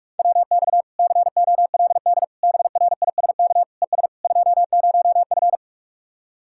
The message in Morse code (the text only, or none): WX COLD BRISK ES 20F